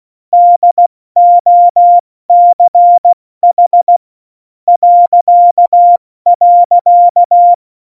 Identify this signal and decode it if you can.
Morse code, heard as a keyed tone, about 16 wpm, spaced as for 14 wpm, 700 Hz. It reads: DOCH ..
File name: Morsekurs / Witze / Witz 13 March 2018, 70 F80